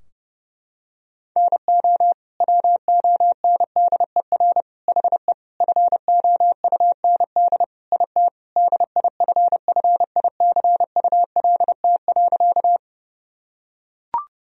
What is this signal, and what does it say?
Morse code, heard as a keyed tone, about 29 wpm, 700 Hz. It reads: NO WONDER HE FOUND IT DIFFICULT.
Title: Sentences from Top 1000 Words - Review Part 3 - 30wpm